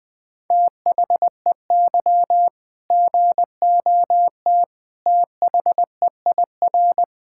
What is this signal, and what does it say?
Morse code, heard as a keyed tone, about 20 wpm, 700 Hz. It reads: THEY GOT THEIR
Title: Sentences from Top 1600 Words 20wpm